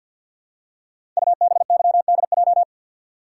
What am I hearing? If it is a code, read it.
Morse code, heard as a keyed tone, about 50 wpm, 700 Hz. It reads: UBYDJ
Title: Code Groups - 5 Letters - 50wpm